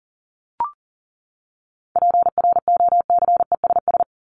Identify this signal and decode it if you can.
Morse code, heard as a keyed tone, about 37 wpm, 700 Hz. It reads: PROCESS
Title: ES:S Suffix - Top 100 Words with Base Words 40wpm